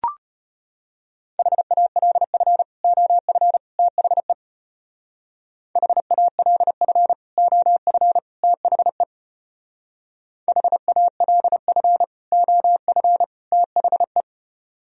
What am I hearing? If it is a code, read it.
Morse code, heard as a keyed tone, about 31 wpm, 700 Hz. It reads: HALF OF THE HALF OF THE HALF OF THE
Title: Speed-Racing - Sets of 3 Words 25wpm